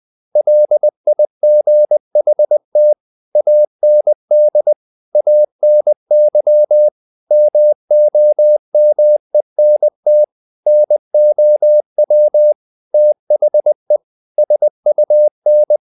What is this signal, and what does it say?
Morse code, heard as a keyed tone, about 20 wpm, 600 Hz. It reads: LIGHT AND ANY MOMENT NOW THE SUN
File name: the old man and the sea / the old man and the sea part 1 20wpm